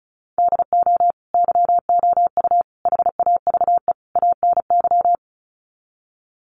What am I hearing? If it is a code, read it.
Morse code, heard as a keyed tone, about 34 wpm, 700 Hz. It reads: DO YOU HAVE ANY
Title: Sets of 4 Words 35wpm